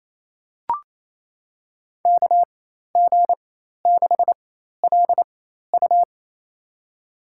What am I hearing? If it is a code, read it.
Morse code, heard as a keyed tone, about 27 wpm, spaced as for 13 wpm, 700 Hz. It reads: KG6LU
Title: USA Call Signs - 5 Characters - 4x Character Spacing - 28wpm